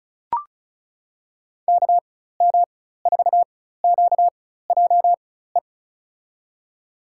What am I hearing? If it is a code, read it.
Morse code, heard as a keyed tone, about 36 wpm, spaced as for 16 wpm, 690 Hz. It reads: KM4QJE
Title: USA Call Signs - 6 Characters - 4x Character Spacing - 35wpm